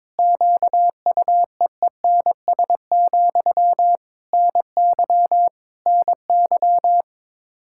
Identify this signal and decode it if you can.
Morse code, heard as a keyed tone, about 22 wpm, 700 Hz. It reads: QUEENS, NY NY
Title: K1USN SST QSO Practice 22wpm